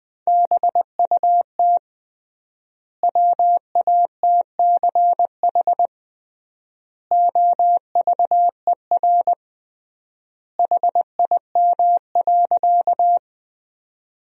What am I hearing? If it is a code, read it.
Morse code, heard as a keyed tone, about 20 wpm, 700 Hz. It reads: BUT WATCH OVER HIM.
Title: 3x Word-Spacing - Sentences from Top 300 Words - 20wpm